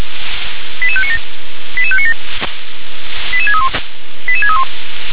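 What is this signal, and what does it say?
Five-tone selective call (ZVEI), heard as quick runs of five tones, tones of about 70 ms. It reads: 88597 88597 88631 88631
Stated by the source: ZVEI five-tone call 88732 received on SDR.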